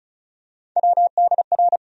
Morse code, heard as a keyed tone, about 36 wpm, 700 Hz. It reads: WDR
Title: USA Call Sign Postfixes - 3 Letters 35wpm